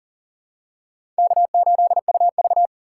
Morse code, heard as a keyed tone, about 40 wpm, 690 Hz. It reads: K8UV
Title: USA Call Signs - 2x Character Spacing - 40wpm